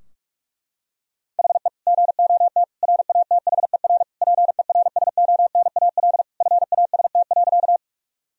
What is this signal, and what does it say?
Morse code, heard as a keyed tone, about 46 wpm, 700 Hz. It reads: HE GOT RATHER PERSONAL FAST.
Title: Sentences from Top 1000 Words - Review Part 4 - 45wpm